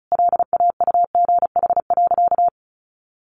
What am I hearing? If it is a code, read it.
Morse code, heard as a keyed tone, about 34 wpm, 700 Hz. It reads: LAUGH.
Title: Sentences from Top 600 Words 35wpm